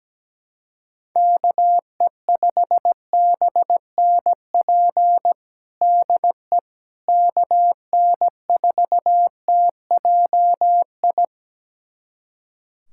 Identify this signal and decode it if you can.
Morse code, heard as a keyed tone, about 17 wpm, 700 Hz. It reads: KE5BNP DE KN4TJI
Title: USA Callsign DE Callsign 17wpm